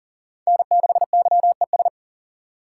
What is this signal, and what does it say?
Morse code, heard as a keyed tone, about 40 wpm, 700 Hz. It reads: N6YES
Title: USA Call Signs - 5 Characters - 2x Character Spacing - 40wpm